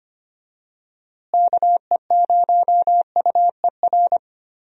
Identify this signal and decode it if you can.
Morse code, heard as a keyed tone, about 25 wpm, 700 Hz. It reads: KE0UER
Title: USA Call Signs - 6 Characters - Repeated 3x - 25wpm